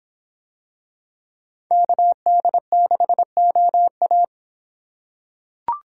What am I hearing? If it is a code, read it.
Morse code, heard as a keyed tone, about 25 wpm, 700 Hz. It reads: KD6OA